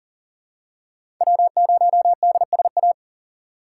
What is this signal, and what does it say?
Morse code, heard as a keyed tone, about 40 wpm, 700 Hz. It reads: W0DSA
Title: USA Call Signs - 5 Characters - 6x Character Spacing - 40wpm